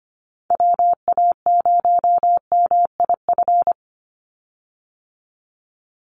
Morse code, heard as a keyed tone, about 25 wpm, 700 Hz. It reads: WA0MIF
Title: USA Call Signs 25wpm